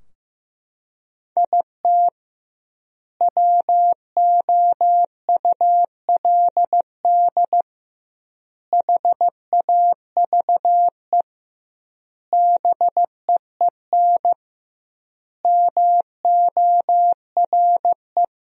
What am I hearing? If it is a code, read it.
Morse code, heard as a keyed tone, about 16 wpm, 700 Hz. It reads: IT WOULD HAVE BEEN MORE
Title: 2x Word-Spacing - Sentences from Top 400 Words - 15wpm